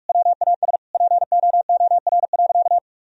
Morse code, heard as a keyed tone, about 46 wpm, 700 Hz. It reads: WAS POOR.
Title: Sentences from Top 1000 Words - Review Part 3 - 45wpm